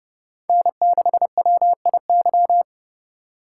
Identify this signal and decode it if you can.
Morse code, heard as a keyed tone, about 32 wpm, 700 Hz. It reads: N6WIY
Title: USA Call Signs - 5 Characters - 2x Character Spacing - 30wpm